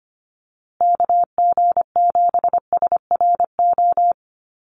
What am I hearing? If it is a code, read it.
Morse code, heard as a keyed tone, about 25 wpm, 700 Hz. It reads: KG7SRO
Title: USA Call Signs Repeated 3x 25wpm